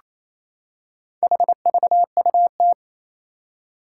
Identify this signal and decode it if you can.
Morse code, heard as a keyed tone, about 27 wpm, 700 Hz. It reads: HVUT
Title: Four Challenging Letter-Number Combinations 28wpm